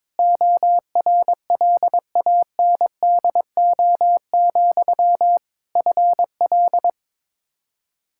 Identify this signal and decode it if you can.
Morse code, heard as a keyed tone, about 22 wpm, 700 Hz. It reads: ORLANDO, FL
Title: Practice QSO - Exchange 1 - Beginning - 22wpm